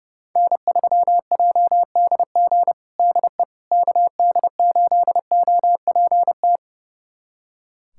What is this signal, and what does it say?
Morse code, heard as a keyed tone, about 30 wpm, 700 Hz. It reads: N3JDG DE KD8OPT
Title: USA Callsign DE Callsign 30wpm